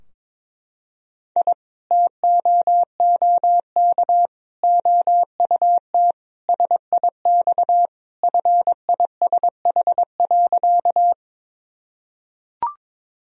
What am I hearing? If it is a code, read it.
Morse code, heard as a keyed tone, about 22 wpm, 700 Hz. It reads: I TOOK OUT SIX FISH.
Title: Sentences from Top 1000 Words - Review Part 2 - 22wpm